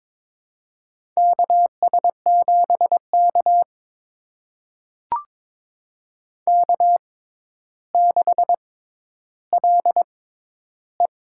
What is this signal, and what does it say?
Morse code, heard as a keyed tone, about 22 wpm, spaced as for 11 wpm, 700 Hz. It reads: KS7K K6LE
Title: USA Call Signs - 4 Characters - 6x Character Spacing - 22wpm